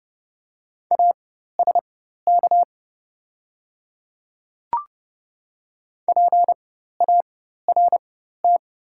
Morse code, heard as a keyed tone, about 31 wpm, spaced as for 14 wpm, 700 Hz. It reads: ASK PART
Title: 4x Character Spacing - Top 200 Words - 30wpm